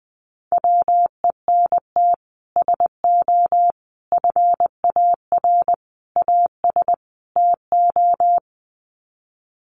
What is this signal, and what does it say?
Morse code, heard as a keyed tone, about 20 wpm, 700 Hz. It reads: WENT SO FAR AS TO
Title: Sets of 5 Words Encore 20wpm